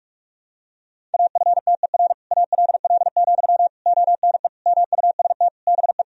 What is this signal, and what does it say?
Morse code, heard as a keyed tone, about 47 wpm, 700 Hz. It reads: AFTER ALL, ONE MUST BE
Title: Sentences from Top 1000 Words - Review Part 3 - 45wpm